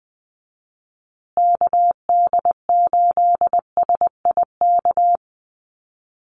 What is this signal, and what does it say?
Morse code, heard as a keyed tone, about 20 wpm, 700 Hz. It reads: KD8SIK
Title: USA Call Signs Repeated 3x 20wpm